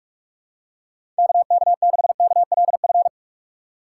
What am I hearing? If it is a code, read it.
Morse code, heard as a keyed tone, about 43 wpm, 700 Hz. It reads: KK6KLF
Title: USA Call Signs - 6 Characters - 6x Character Spacing - 45wpm